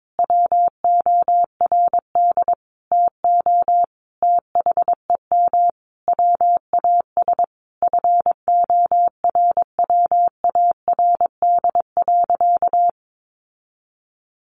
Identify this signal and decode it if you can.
Morse code, heard as a keyed tone, about 22 wpm, 700 Hz. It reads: WORD TO THEM WAS FORWARD.
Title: Sentences from Top 1500 Words 22wpm